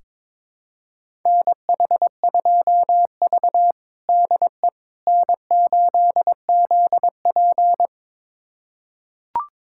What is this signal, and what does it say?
Morse code, heard as a keyed tone, about 23 wpm, 700 Hz. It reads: NH2V DE N8ZP
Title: USA Callsign DE Callsign - 4 Characters - 22wpm